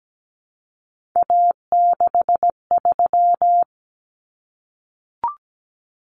Morse code, heard as a keyed tone, about 17 wpm, 700 Hz. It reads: A63
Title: Common Cut Numbers - 3-Digit 17wpm